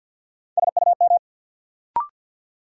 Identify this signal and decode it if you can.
Morse code, heard as a keyed tone, about 51 wpm, 700 Hz. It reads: SUM
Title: Sentences from Top 1000 Words - Review Part 6 - New Word Drill - 50wpm